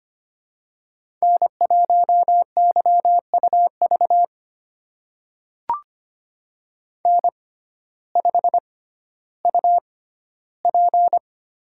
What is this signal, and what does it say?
Morse code, heard as a keyed tone, about 25 wpm, spaced as for 14 wpm, 700 Hz. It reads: N1YUV N5UP
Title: USA Call Signs - 6x Character Spacing - 25wpm